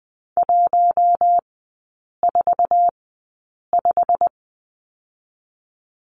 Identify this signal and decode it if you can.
Morse code, heard as a keyed tone, about 20 wpm, spaced as for 8 wpm, 690 Hz. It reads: 145
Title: 087 Lesson 15e - Number 5 - Three Characters 20wpm